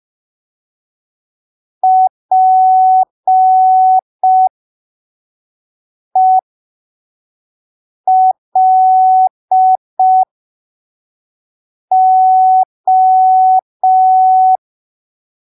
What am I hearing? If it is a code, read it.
Morse code, heard as a keyed tone, about 5 wpm, spaced as for 3 wpm, 750 Hz. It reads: PELO
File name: arquivos / texto / texto 01 5ppm